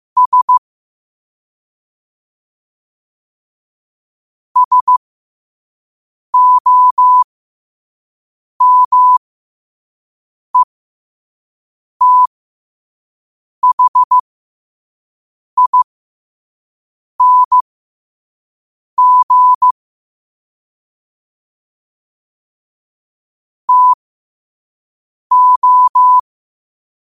Morse code, heard as a keyed tone, about 15 wpm, spaced as for 5 wpm, 1000 Hz. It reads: S SOMETHING TO
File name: qotdfiles / 5wpm